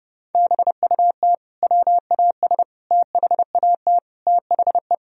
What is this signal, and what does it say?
Morse code, heard as a keyed tone, about 30 wpm, 700 Hz. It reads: BUT WAS THAT THE
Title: Sentences from Top 1000 Words - Review Part 2 - 30wpm